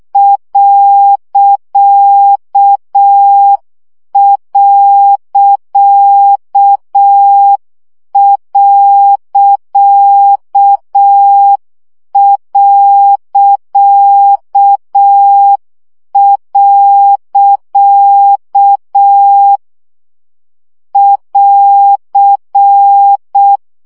Morse code, heard as a keyed tone, about 6 wpm, 800 Hz. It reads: ..... <AR>
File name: CW gyakorlas pont 5 baud